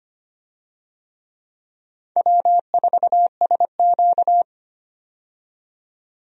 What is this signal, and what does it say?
Morse code, heard as a keyed tone, about 25 wpm, 700 Hz. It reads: W4SQ
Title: USA Call Signs - 4 Characters - 25wpm